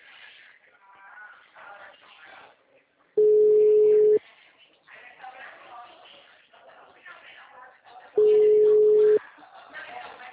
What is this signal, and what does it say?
Ringback tone (Europe, 425 Hz), with speech.